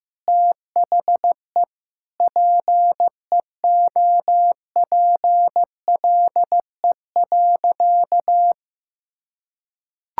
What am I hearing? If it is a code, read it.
Morse code, heard as a keyed tone, about 16 wpm, 700 Hz. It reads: THE PEOPLE.
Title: Sentences from Top 1600 Words 15wpm